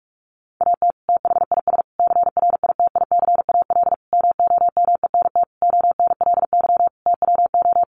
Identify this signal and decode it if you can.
Morse code, heard as a keyed tone, about 44 wpm, 700 Hz. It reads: AT THIS CRITICAL MOMENT ONLY TWO
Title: Sentences from Top 1100 Words 45wpm